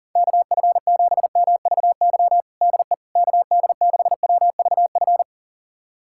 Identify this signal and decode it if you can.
Morse code, heard as a keyed tone, about 40 wpm, 700 Hz. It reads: KF7MVY DE KD6WVF
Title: USA Callsign DE Callsign - 6 Characters - 40wpm